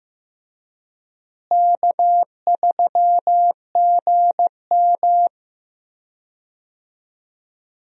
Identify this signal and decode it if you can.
Morse code, heard as a keyed tone, about 15 wpm, 700 Hz. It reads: K3GM
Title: USA Call Signs - 4 Characters - Repeated 3x - 15wpm